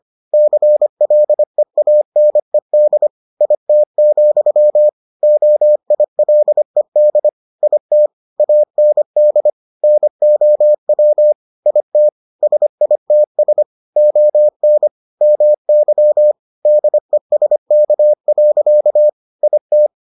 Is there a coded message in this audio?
Yes: Morse code from 0.3 to 19.9 s, heard as a keyed tone, about 25 wpm, 600 Hz. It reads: CLEANED IT, OILED IT AND NOW IT SITS ON MY DESK. IT